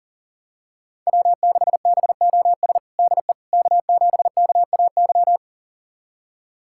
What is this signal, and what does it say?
Morse code, heard as a keyed tone, about 37 wpm, 690 Hz. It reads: W6BOS DE K7KAY
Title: USA Callsign DE Callsign - 5 Characters - 40wpm